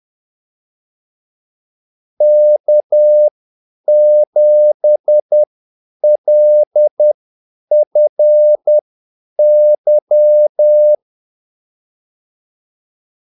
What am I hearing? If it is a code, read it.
Morse code, heard as a keyed tone, about 10 wpm, spaced as for 8 wpm, 600 Hz. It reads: K7LFY
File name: Callsigns-10-8wpm